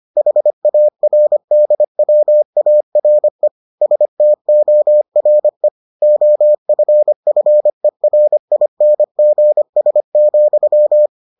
Morse code, heard as a keyed tone, about 25 wpm, 600 Hz. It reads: HARDWARE STORE OFFERINGS,